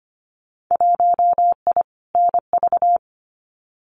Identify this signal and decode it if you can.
Morse code, heard as a keyed tone, about 25 wpm, 700 Hz. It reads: 1I NV
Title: Winter Field Day Exchanges Encore 25wpm